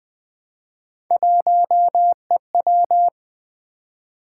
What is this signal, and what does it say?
Morse code, heard as a keyed tone, about 20 wpm, 700 Hz. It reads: 1EW
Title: Three Letter-Number Combinations - 20wpm